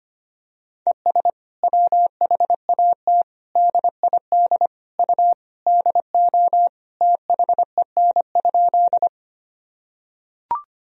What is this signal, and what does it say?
Morse code, heard as a keyed tone, about 25 wpm, 700 Hz. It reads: ES WHAT DID U DO THEN?
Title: Sentences from Top 100 Words and Abbreviations 25wpm